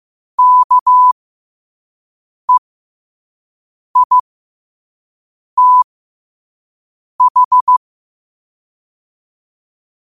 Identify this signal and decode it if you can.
Morse code, heard as a keyed tone, about 15 wpm, spaced as for 5 wpm, 1000 Hz. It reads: KEITH
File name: qotdfiles / 5wpm